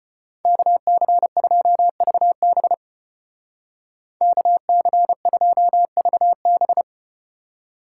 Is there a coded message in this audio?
Yes: Morse code from 0.4 to 6.8 s, heard as a keyed tone, about 33 wpm, 700 Hz. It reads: KC2VB KC2VB